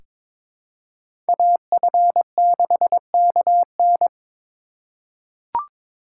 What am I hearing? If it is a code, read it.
Morse code, heard as a keyed tone, about 22 wpm, 700 Hz. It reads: AF6KN
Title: USA Call Signs - 5 Characters - 4x Character Spacing - 22wpm